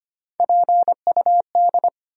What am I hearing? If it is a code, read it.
Morse code, heard as a keyed tone, about 25 wpm, 700 Hz. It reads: PUD
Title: USA Call Sign Postfixes - 3 Letters 25wpm